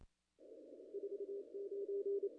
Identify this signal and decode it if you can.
Morse code, heard as a keyed tone, about 27 wpm, 410 Hz. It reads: 49
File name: contests / ss-cw-k5tr-op-n5rz -0000